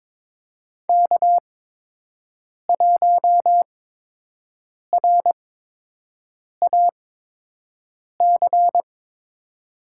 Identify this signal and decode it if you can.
Morse code, heard as a keyed tone, about 23 wpm, spaced as for 6 wpm, 700 Hz. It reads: K1RAC
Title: USA Call Signs - 5 Characters - 8x Character Spacing - 22wpm